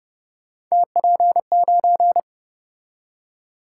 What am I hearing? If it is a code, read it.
Morse code, heard as a keyed tone, about 33 wpm, 700 Hz. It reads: TP9